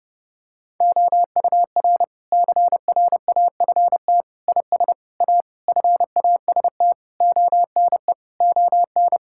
Morse code, heard as a keyed tone, about 30 wpm, 700 Hz. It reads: OUR CRAFT IS A FAST ONE ON
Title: Sentences from Top 1000 Words - Review Part 6 - 30wpm